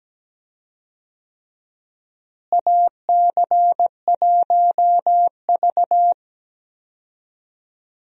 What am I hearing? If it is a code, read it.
Morse code, heard as a keyed tone, about 17 wpm, 700 Hz. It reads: AC1V
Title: USA Call Signs - 4 Characters - Repeated 3x - 17wpm